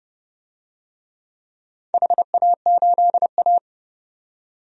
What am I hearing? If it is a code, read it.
Morse code, heard as a keyed tone, about 29 wpm, 700 Hz. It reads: HA8A
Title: DXCC-100 Call Signs - 4 Characters - 30wpm